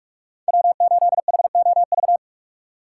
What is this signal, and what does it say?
Morse code, heard as a keyed tone, about 41 wpm, 700 Hz. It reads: W8HOV
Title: USA Call Signs - 5 Characters - 4x Character Spacing - 45wpm